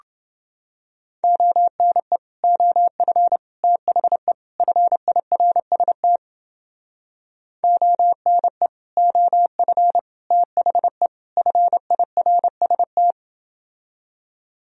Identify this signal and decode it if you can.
Morse code, heard as a keyed tone, about 30 wpm, 700 Hz. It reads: ONE OF THE FIRST ONE OF THE FIRST